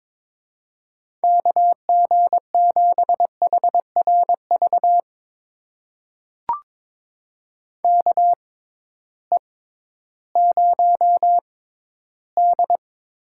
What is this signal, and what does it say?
Morse code, heard as a keyed tone, about 22 wpm, spaced as for 13 wpm, 700 Hz. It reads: KG7HRV KE0D
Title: USA Call Signs - 6 Characters - 6x Character Spacing - 22wpm